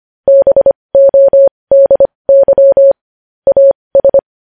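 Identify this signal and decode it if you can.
Morse code, heard as a keyed tone, about 25 wpm, spaced as for 20 wpm, 550 Hz. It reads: BODY AS
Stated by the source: Morse code news headlines.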